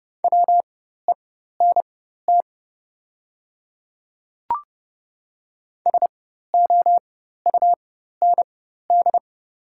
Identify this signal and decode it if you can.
Morse code, heard as a keyed tone, about 30 wpm, spaced as for 14 wpm, 700 Hz. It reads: WENT SOUND
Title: 4x Character Spacing - Top 300 Words - 30wpm